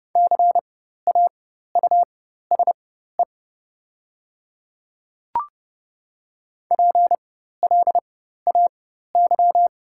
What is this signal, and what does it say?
Morse code, heard as a keyed tone, about 30 wpm, spaced as for 14 wpm, 690 Hz. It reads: CAUSE PLAY
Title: 4x Character Spacing - Top 500 Words - 30wpm